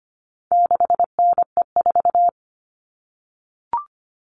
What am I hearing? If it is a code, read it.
Morse code, heard as a keyed tone, about 25 wpm, 700 Hz. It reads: 6NE4
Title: All Cut Numbers - 4-Digit 25wpm